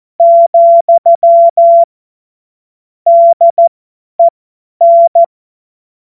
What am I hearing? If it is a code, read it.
Morse code, heard as a keyed tone, about 14 wpm, spaced as for 10 wpm, 680 Hz. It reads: , DEN
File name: Morsekurs / Witze / Witz 25. Juli 2023 50 F70